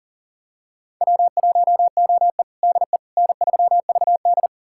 Morse code, heard as a keyed tone, about 37 wpm, 700 Hz. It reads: W1OE DE N3VD
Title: USA Callsign DE Callsign - 4 Characters - 40wpm